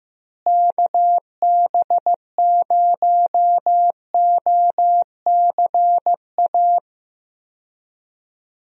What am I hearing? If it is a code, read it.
Morse code, heard as a keyed tone, about 16 wpm, 700 Hz. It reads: KB0OCA
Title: USA Call Signs - 6 Characters - Repeated 2x - 15wpm